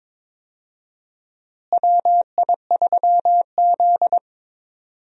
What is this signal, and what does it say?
Morse code, heard as a keyed tone, about 21 wpm, 700 Hz. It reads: WI3Z
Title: USA Call Signs - 4 Characters - Repeated 3x - 22wpm